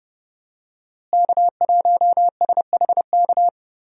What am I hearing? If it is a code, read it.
Morse code, heard as a keyed tone, about 31 wpm, 700 Hz. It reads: K1SHK
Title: USA Call Signs - 5 Characters - Repeated 2x - 30wpm